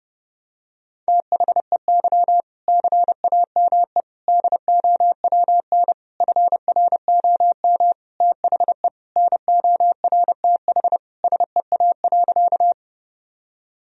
Morse code, heard as a keyed tone, about 32 wpm, 700 Hz. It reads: THEY CAME DOWN FROM THE NORTH SEA.
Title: Sentences from Top 1000 Words - Review Part 1 - 30wpm